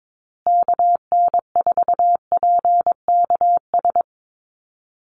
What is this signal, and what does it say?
Morse code, heard as a keyed tone, about 22 wpm, 700 Hz. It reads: KN4PKS
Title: USA Call Signs Repeated 2x 22wpm